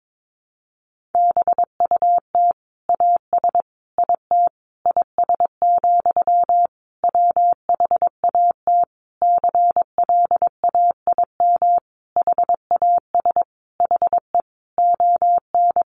Morse code, heard as a keyed tone, about 22 wpm, 700 Hz. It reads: BUT AS IT IS, WHAT CLAIM HAS HE ON
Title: Sentences from Top 1500 Words 22wpm